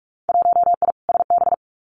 Morse code, heard as a keyed tone, about 44 wpm, 700 Hz. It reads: 1I SB
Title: Winter Field Day Exchanges 45wpm